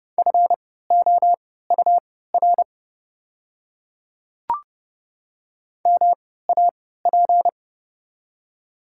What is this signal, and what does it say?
Morse code, heard as a keyed tone, about 30 wpm, spaced as for 17 wpm, 700 Hz. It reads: FOUR MAP
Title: 3x Character Spacing - Top 500 Words - 30wpm